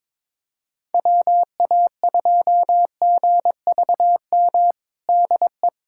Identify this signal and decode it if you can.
Morse code, heard as a keyed tone, about 21 wpm, 700 Hz. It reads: WA2GVM DE
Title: USA Callsign DE Callsign - 6 Characters - 22wpm